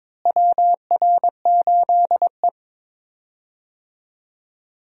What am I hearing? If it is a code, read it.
Morse code, heard as a keyed tone, about 22 wpm, 700 Hz. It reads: WR8E
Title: USA Call Signs - 4 Characters - Repeated 3x - 22wpm